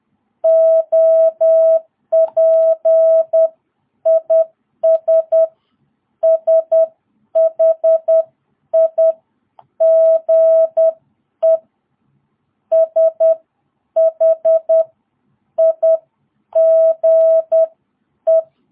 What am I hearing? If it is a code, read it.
Morse code, heard as a keyed tone, about 10 wpm, spaced as for 8 wpm, 660 Hz. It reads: OP IS SHIGE SHIGE
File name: jh8cbh / 2022cw